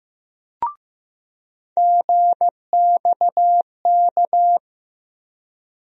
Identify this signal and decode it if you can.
Morse code, heard as a keyed tone, about 15 wpm, 700 Hz. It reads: GXK
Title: USA Call Sign Postfixes - 3 Letters 15wpm